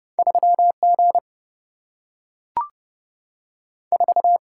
Morse code, heard as a keyed tone, about 30 wpm, 700 Hz. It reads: 3 G 4
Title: Single-Letter Single-Number 30wpm